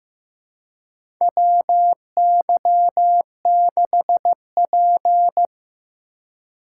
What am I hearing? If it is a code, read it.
Morse code, heard as a keyed tone, about 15 wpm, 700 Hz. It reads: WY6P
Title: USA Call Signs - 4 Characters - 3x Character Spacing - 15wpm